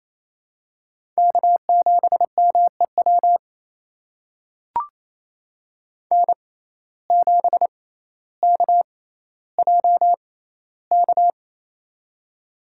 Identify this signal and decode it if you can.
Morse code, heard as a keyed tone, about 28 wpm, spaced as for 14 wpm, 700 Hz. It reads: K7MEW N7KJK
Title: USA Call Signs - 6x Character Spacing - 28wpm